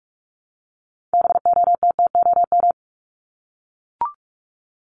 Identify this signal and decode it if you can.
Morse code, heard as a keyed tone, about 44 wpm, 700 Hz. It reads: BOTTOM
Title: Top 751-1000 Words 45wpm